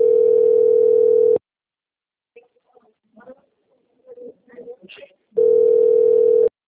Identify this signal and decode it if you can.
Ringback tone (North America), 440 + 480 Hz.